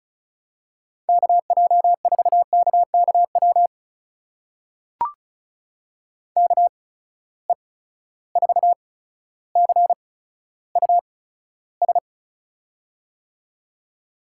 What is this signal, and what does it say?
Morse code, heard as a keyed tone, about 36 wpm, spaced as for 15 wpm, 700 Hz. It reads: KJ4KKW KE4CUS
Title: USA Call Signs - 6 Characters - 8x Character Spacing - 35wpm